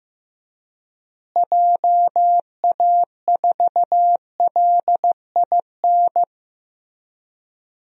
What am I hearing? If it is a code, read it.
Morse code, heard as a keyed tone, about 15 wpm, 700 Hz. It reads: JA4LIN
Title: DXCC-100 Call Signs - 6 Characters - Repeated 2x - 15wpm